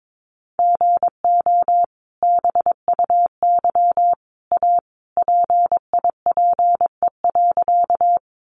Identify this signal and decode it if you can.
Morse code, heard as a keyed tone, about 22 wpm, 700 Hz. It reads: GO BUY A PIPE.
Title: Sentences from Top 1500 Words 22wpm